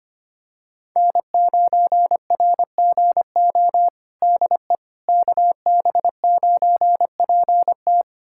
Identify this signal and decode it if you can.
Morse code, heard as a keyed tone, about 25 wpm, 700 Hz. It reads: N9RGO DE KB9PT
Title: USA Callsign DE Callsign - 5 Characters - 25wpm